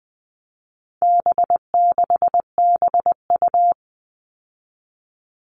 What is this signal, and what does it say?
Morse code, heard as a keyed tone, about 20 wpm, 700 Hz. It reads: B6BU
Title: All Cut Numbers - 4-Digit 20wpm